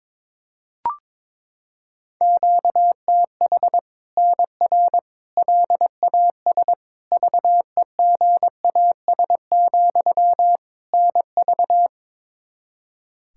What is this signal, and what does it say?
Morse code, heard as a keyed tone, about 22 wpm, 700 Hz. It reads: QTH NR LAS VEGAS, NV
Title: Practice QSO - Exchange 1 - Beginning - 22wpm